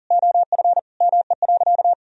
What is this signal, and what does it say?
Morse code, heard as a keyed tone, about 37 wpm, 700 Hz. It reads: OF ME.
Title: Sentences from Top 1000 Words - Review Part 2 - 40wpm